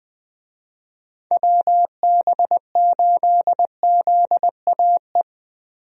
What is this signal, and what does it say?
Morse code, heard as a keyed tone, about 21 wpm, 700 Hz. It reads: WB8ZAE
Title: USA Call Signs - 6 Characters - 20wpm